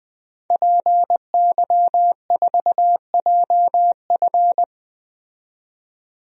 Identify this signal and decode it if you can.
Morse code, heard as a keyed tone, about 20 wpm, 700 Hz. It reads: PY4JF
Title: DXCC-100 Call Signs - 5 Characters - 20wpm